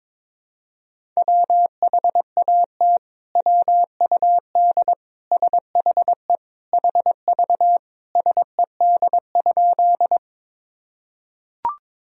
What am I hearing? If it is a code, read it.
Morse code, heard as a keyed tone, about 22 wpm, 700 Hz. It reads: WHAT WUD SHE HV SED?
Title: Sentences from Top 100 Words and Abbreviations 22wpm